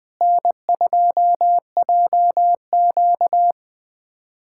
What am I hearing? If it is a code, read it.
Morse code, heard as a keyed tone, about 20 wpm, 700 Hz. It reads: N2JQ
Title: USA Callsign DE Callsign - 4 Characters - 20wpm